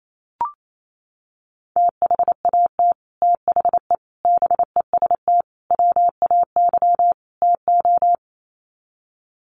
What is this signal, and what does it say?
Morse code, heard as a keyed tone, about 28 wpm, 700 Hz. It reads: THAT THE BEST WAY TO
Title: Sets of 5 Words Encore2 28wpm